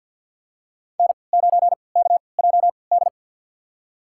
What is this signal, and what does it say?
Morse code, heard as a keyed tone, about 48 wpm, spaced as for 28 wpm, 700 Hz. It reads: N9KJD
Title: USA Call Signs - 5 Characters - 3x Character Spacing - 50wpm